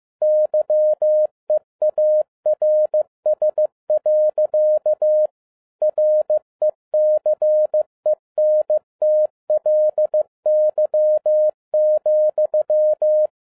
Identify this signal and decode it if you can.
Morse code, heard as a keyed tone, about 15 wpm, 610 Hz. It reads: YEARS. RECENTLY,